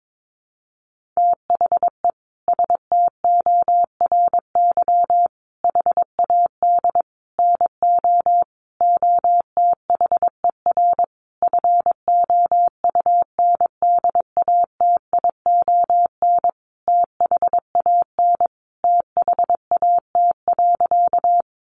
Morse code, heard as a keyed tone, about 22 wpm, 700 Hz. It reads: THE STORY HAD NO OTHER FOUNDATION THAN THAT.
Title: Sentences from Top 1200 Words 22wpm